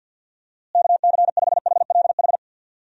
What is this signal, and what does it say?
Morse code, heard as a keyed tone, about 52 wpm, 700 Hz. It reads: KC5HLH
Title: USA Call Signs - 6 Characters - Repeated 2x - 50wpm